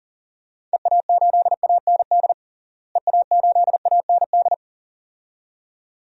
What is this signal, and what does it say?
Morse code, heard as a keyed tone, about 37 wpm, 700 Hz. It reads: EA8AND EA8AND
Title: DXCC-100 Call Signs - 6 Characters - Repeated 2x - 40wpm